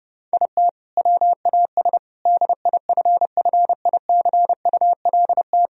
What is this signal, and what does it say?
Morse code, heard as a keyed tone, about 30 wpm, 700 Hz. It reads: IT WAS DIFFICULT
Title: Sentences from Top 1000 Words - Review Part 1 - 30wpm